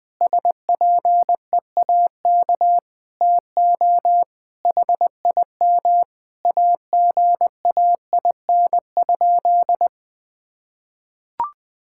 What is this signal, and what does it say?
Morse code, heard as a keyed tone, about 20 wpm, 700 Hz. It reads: SPEAK TO HIM AGAIN?
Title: Sentences from Top 1600 Words 20wpm